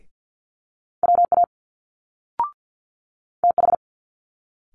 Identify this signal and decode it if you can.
Morse code, heard as a keyed tone, about 50 wpm, 700 Hz. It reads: WA TH
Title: Multi-letter Phonemes Quick Fire 50wpm